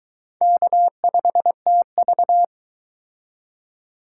Morse code, heard as a keyed tone, about 23 wpm, 700 Hz. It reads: K5TV